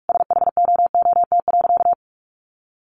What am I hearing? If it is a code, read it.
Morse code, heard as a keyed tone, about 44 wpm, 700 Hz. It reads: SHOOT.
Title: A Sound Of Thunder 45wpm